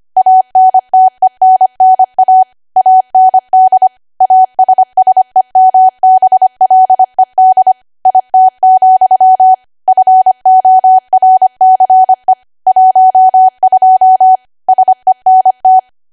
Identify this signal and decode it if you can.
Morse code, heard as a keyed tone, about 25 wpm, 750 Hz. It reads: ANTENNA AND ASSEMBLED IT, FORCE 12 SENT
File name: Archive / 25 WPM / 25WPM